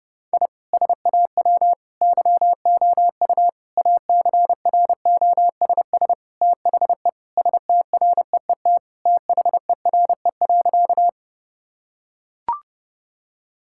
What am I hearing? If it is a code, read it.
Morse code, heard as a keyed tone, about 30 wpm, 700 Hz. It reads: I SAW YOU ACROSS THE STREET THERE.
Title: Sentences from Top 1000 Words - Review Part 5 - 30wpm